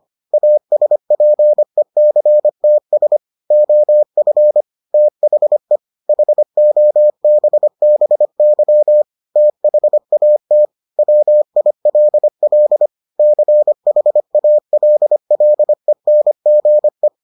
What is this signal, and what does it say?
Morse code, heard as a keyed tone, about 25 wpm, 600 Hz. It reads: ASPECTS OF THE HOBBY THAT WILL CHALLENGE